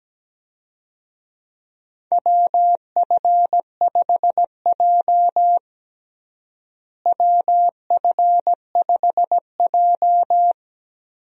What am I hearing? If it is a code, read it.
Morse code, heard as a keyed tone, about 17 wpm, 700 Hz. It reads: WF5J WF5J
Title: USA Call Signs - 4 Characters - Repeated 2x - 17wpm